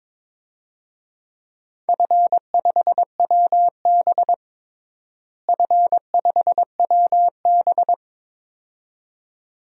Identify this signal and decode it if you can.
Morse code, heard as a keyed tone, about 22 wpm, 700 Hz. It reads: F5WB F5WB
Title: DXCC-100 Call Signs - 4 Characters - Repeated 2x - 22wpm